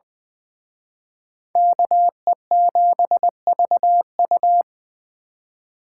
Morse code, heard as a keyed tone, about 23 wpm, 700 Hz. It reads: KE7VU
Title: Speed Racing - USA 2x2 Call Signs 20wpm